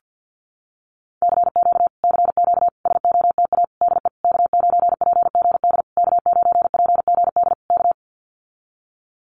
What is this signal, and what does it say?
Morse code, heard as a keyed tone, about 49 wpm, 700 Hz. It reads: CQ CQ SOTA DE K9PGD K9PGD K
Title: Common QSO Phrases Encore 50wpm